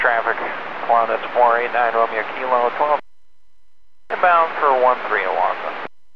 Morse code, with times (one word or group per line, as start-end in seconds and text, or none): none